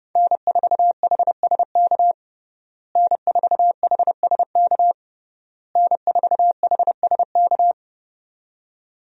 Morse code, heard as a keyed tone, about 30 wpm, 700 Hz. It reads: N4HSK N4HSK N4HSK
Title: USA Call Signs - 5 Characters - Repeated 3x - 30wpm